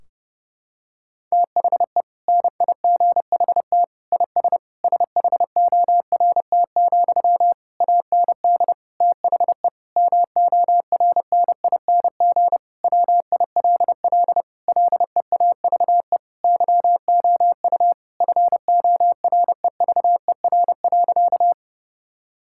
Morse code, heard as a keyed tone, about 32 wpm, 700 Hz. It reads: THE NIGHT IS SHORT, AND THE MORNING WILL LEAVE YOU FOREVER.
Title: Sentences from Top 1600 Words 30wpm